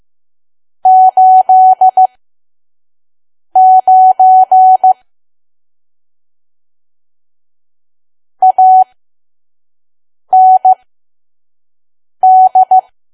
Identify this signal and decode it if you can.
Morse code, heard as a keyed tone, about 15 wpm, spaced as for 5 wpm, 750 Hz. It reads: 89 AND
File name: Archive / 5 WPM files / PM